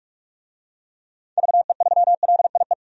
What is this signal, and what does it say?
Morse code, heard as a keyed tone, about 43 wpm, 700 Hz. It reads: VE3LIE
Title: DXCC-100 Call Signs - 6 Characters - 45wpm